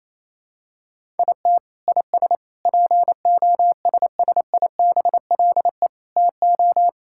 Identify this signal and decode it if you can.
Morse code, heard as a keyed tone, about 28 wpm, 700 Hz. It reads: IT IS POSSIBLE TO
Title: Sentences from Top 1000 Words - Review Part 6 - 28wpm